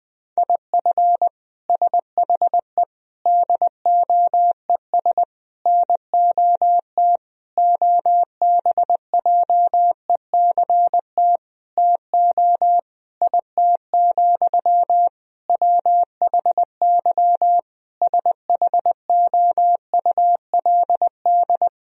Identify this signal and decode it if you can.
Morse code, heard as a keyed tone, about 20 wpm, 700 Hz. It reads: IF SHE DOES NOT OBJECT TO IT, WHY SHOULD
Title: Sentences from Top 1000 Words - Review Part 3 - 20wpm